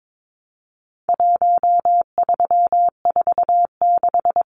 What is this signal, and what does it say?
Morse code, heard as a keyed tone, about 22 wpm, 700 Hz. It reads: 1346
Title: Numbers - Contest Style - 4-Digit 22wpm